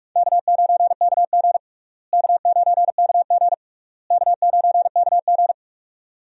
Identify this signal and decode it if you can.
Morse code, heard as a keyed tone, about 42 wpm, 700 Hz. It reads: K9KG K9KG K9KG
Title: USA Call Signs - 4 Characters - Repeated 3x - 45wpm